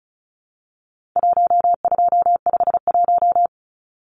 Morse code, heard as a keyed tone, about 34 wpm, 700 Hz. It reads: 1251
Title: Numbers - Contest Style - 4-Digit 35wpm